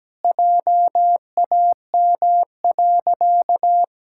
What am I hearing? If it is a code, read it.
Morse code, heard as a keyed tone, about 17 wpm, 700 Hz. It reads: JAM.
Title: Pangram Challenge v2 - 17wpm